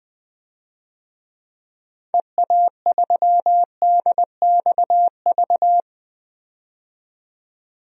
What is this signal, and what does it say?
Morse code, heard as a keyed tone, about 21 wpm, 700 Hz. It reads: EA3DXV
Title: DXCC-100 Call Signs - 6 Characters - 20wpm